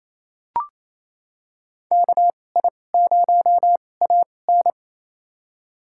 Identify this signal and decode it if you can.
Morse code, heard as a keyed tone, about 28 wpm, spaced as for 20 wpm, 700 Hz. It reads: KI0AN
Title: USA Call Signs - 2x Character Spacing - 28wpm